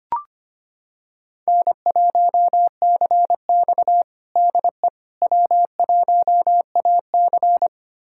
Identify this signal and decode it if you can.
Morse code, heard as a keyed tone, about 25 wpm, 700 Hz. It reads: N1CX DE W1AC
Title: USA Callsign DE Callsign - 4 Characters - 25wpm